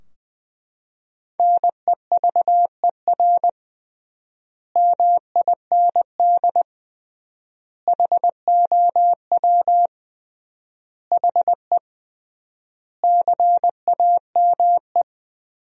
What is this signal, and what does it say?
Morse code, heard as a keyed tone, about 20 wpm, 700 Hz. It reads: NEVER MIND HOW HE CAME
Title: 3x Word-Spacing - Sentences from Top 500 Words - 20wpm